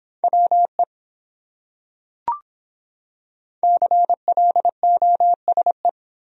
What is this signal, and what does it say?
Morse code, heard as a keyed tone, about 24 wpm, 700 Hz. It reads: WE CLOSE